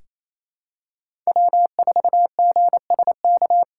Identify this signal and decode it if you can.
Morse code, heard as a keyed tone, about 28 wpm, 700 Hz. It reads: W4GSK
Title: USA Callsign DE Callsign 28wpm